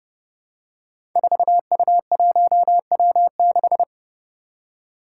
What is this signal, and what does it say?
Morse code, heard as a keyed tone, about 30 wpm, 700 Hz. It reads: 4U1W6
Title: Five Challenging Letter-Number Combinations 30wpm